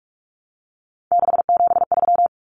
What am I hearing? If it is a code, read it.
Morse code, heard as a keyed tone, about 44 wpm, 700 Hz. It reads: 673
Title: Common Cut Numbers - 3-Digit 45wpm